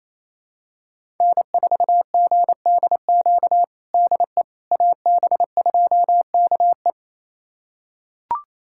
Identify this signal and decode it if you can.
Morse code, heard as a keyed tone, about 28 wpm, 700 Hz. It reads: N4GDQ DE AB2KE
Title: USA Callsign DE Callsign - 5 Characters - 28wpm